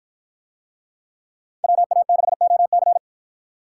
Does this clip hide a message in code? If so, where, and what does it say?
Yes: Morse code from 1.6 to 3.0 s, heard as a keyed tone, about 48 wpm, 700 Hz. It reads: WA6OC